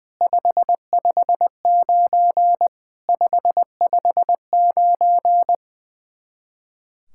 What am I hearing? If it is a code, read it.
Morse code, heard as a keyed tone, about 21 wpm, 700 Hz. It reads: 559 559
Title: Practice QSO - Exchange 1 - Beginning - 20wpm